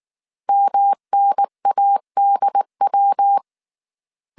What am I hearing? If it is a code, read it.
Morse code, heard as a keyed tone, about 19 wpm, 800 Hz. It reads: MNADW